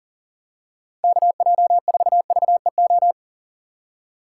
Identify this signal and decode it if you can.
Morse code, heard as a keyed tone, about 38 wpm, 700 Hz. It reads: KJ4VEO
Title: USA Call Signs - 6 Characters - 2x Character Spacing - 40wpm